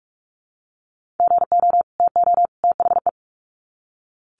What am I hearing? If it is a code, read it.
Morse code, heard as a keyed tone, about 44 wpm, 700 Hz. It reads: GO TO THE
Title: Sets of 3 Words 45wpm